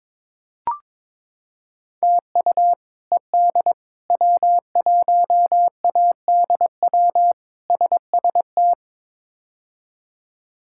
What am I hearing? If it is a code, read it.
Morse code, heard as a keyed tone, about 22 wpm, 700 Hz. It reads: TU ED W1ADW SST
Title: K1USN SST QSO Practice - 22wpm